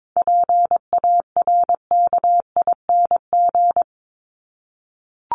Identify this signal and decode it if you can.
Morse code, heard as a keyed tone, about 22 wpm, 700 Hz. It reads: PARKING
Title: Top 1501-1750 Words 22wpm